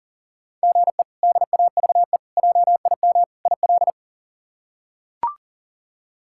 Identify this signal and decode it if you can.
Morse code, heard as a keyed tone, about 40 wpm, 700 Hz. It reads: GE DAVE JIM IL
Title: K1USN SST QSO Practice - 40wpm